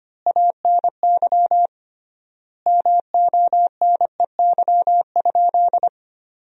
Morse code, heard as a keyed tone, about 25 wpm, 700 Hz. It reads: ANY MONEY?
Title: 3x Word-Spacing - Sentences from Top 400 Words - 25wpm